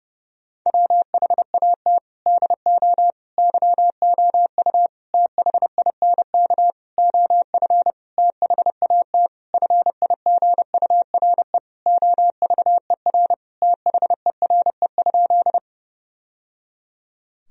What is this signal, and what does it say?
Morse code, heard as a keyed tone, about 32 wpm, 700 Hz. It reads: WHAT DO YOU THINK OF THAT FIGURE OVER THERE?
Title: Sentences from Top 1000 Words - Review Part 2 - 30wpm